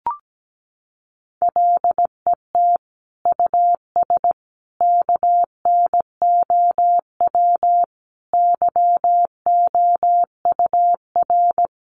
Morse code, heard as a keyed tone, about 17 wpm, 700 Hz. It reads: LET US KNOW YOUR
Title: Sentences from Top 1400 Words 17wpm